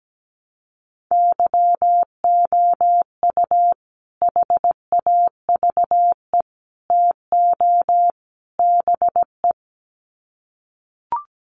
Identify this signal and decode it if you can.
Morse code, heard as a keyed tone, about 17 wpm, 700 Hz. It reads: YOU HAVE TO BE
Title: Sets of 4 Words 17wpm